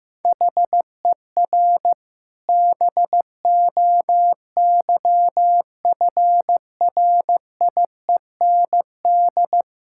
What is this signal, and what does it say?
Morse code, heard as a keyed tone, about 15 wpm, 700 Hz. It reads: HER BOYFRIEND
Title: Sentences from Top 1000 Words - Review Part 5 - 15wpm